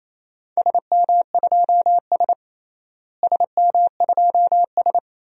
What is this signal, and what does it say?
Morse code, heard as a keyed tone, about 27 wpm, 700 Hz. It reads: SM2S SM2S
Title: DXCC-100 Call Signs - 4 Characters - Repeated 2x - 28wpm